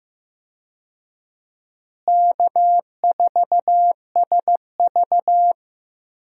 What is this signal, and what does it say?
Morse code, heard as a keyed tone, about 16 wpm, 700 Hz. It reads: K4SV
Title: USA Call Signs - 4 Characters - Repeated 2x - 15wpm